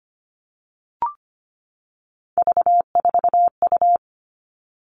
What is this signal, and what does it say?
Morse code, heard as a keyed tone, about 25 wpm, 700 Hz. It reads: V4U
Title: All Cut Numbers - 3-Digit 25wpm